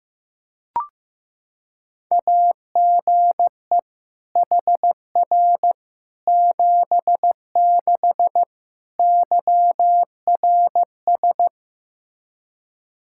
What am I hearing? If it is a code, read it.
Morse code, heard as a keyed tone, about 16 wpm, 700 Hz. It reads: AGE HR 76 YRS
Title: Practice QSO - Exchange 3 - More Info - 15wpm